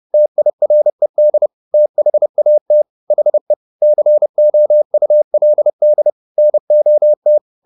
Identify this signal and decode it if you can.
Morse code, heard as a keyed tone, about 30 wpm, 600 Hz. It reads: TIRED THAT HE COULD NOT
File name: the old man and the sea / the old man and the sea 1 30wpm